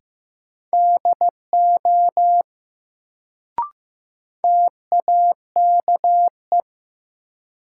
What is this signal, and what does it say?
Morse code, heard as a keyed tone, about 15 wpm, 700 Hz. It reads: DO TAKE
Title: Sets of 2 Words - New Word Drill - 15wpm